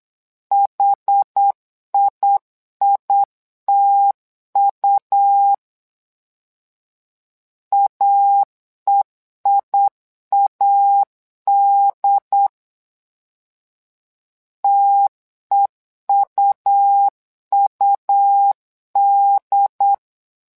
Morse code, heard as a keyed tone, about 8 wpm, 800 Hz. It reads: HIITU AEIAD TEUUD
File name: LEZIONI / 2 UHDN / 2 UHD 8wpm 2258